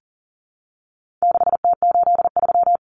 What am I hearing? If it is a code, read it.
Morse code, heard as a keyed tone, about 40 wpm, 700 Hz. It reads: 6T83
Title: Common Cut Numbers - 4-Digit 40wpm